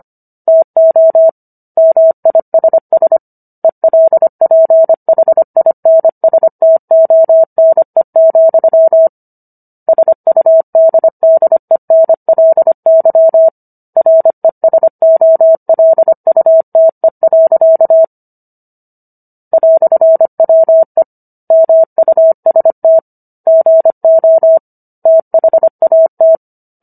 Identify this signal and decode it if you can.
Morse code, heard as a keyed tone, about 25 wpm, 650 Hz. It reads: TO MISS ELPHINSTONE, SUDDENLY RESOLUTE. "WE MUST GO THAT